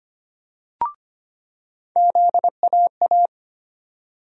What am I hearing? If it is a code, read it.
Morse code, heard as a keyed tone, about 25 wpm, 700 Hz. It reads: ZAA
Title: USA Call Sign Postfixes - 3 Letters 25wpm